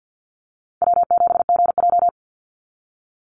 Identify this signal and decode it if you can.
Morse code, heard as a keyed tone, about 50 wpm, 700 Hz. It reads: W7GJ
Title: NA SOTA Chaser Callsigns 50wpm